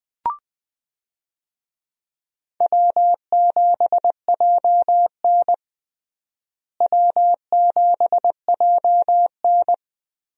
Morse code, heard as a keyed tone, about 20 wpm, 700 Hz. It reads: W7JN W7JN
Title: USA Call Signs - 4 Characters - Repeated 3x - 20wpm